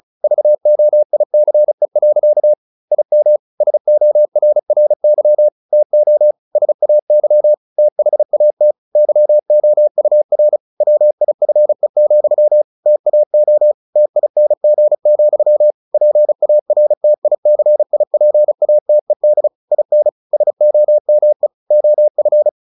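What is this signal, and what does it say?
Morse code, heard as a keyed tone, about 34 wpm, 600 Hz. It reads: VOICE. IM SORRY TO SAY THAT YOUR WIFE, TAO TING, PARTICIPATED IN SOME OF
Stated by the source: Morse practice at 35 wpm from CW med Gnister, sending Ken Follett's novel Never in English.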